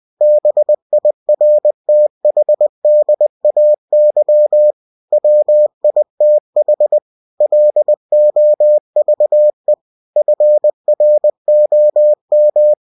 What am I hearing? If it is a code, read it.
Morse code, heard as a keyed tone, about 20 wpm, 600 Hz. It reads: BIRTHDAY WITH LOVE FROM
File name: cw-books / winnie-the-pooh / winnie-20wpm-page-0057